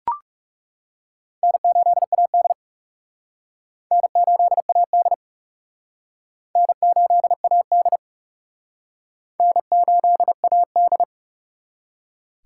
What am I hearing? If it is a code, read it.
Morse code, heard as a keyed tone, about 36 wpm, 700 Hz. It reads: N8AD N8AD N8AD N8AD
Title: Speed Racing - USA 1x2 Call Signs 30wpm